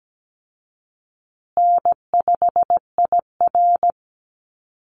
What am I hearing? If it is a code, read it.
Morse code, heard as a keyed tone, about 17 wpm, 700 Hz. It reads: N5IR
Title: Call Signs Repeated 3x - 17wpm